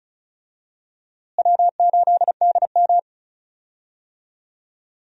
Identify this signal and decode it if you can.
Morse code, heard as a keyed tone, about 36 wpm, 700 Hz. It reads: W8DM
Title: USA Call Signs - 4 Characters - Repeated 3x - 35wpm